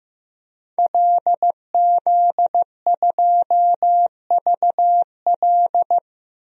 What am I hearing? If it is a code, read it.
Morse code, heard as a keyed tone, about 15 wpm, 700 Hz. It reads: LZ2VL
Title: DXCC-100 Call Signs - 5 Characters - Repeated 3x - 15wpm